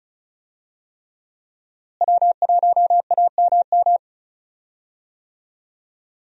Morse code, heard as a keyed tone, about 36 wpm, 700 Hz. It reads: W1AMM
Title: USA Call Signs - 5 Characters - Repeated 3x - 35wpm